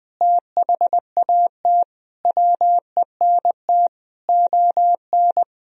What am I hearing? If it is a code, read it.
Morse code, heard as a keyed tone, about 20 wpm, 700 Hz. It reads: THAT WENT ON
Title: Sentences from Top 1000 Words - Review Part 3 - 20wpm